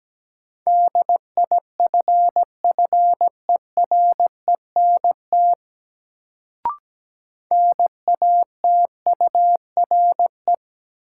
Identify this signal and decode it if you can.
Morse code, heard as a keyed tone, about 17 wpm, 700 Hz. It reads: DIFFERENT NATURE
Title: Sentences from Top 600 Words - New Word Drill - 17wpm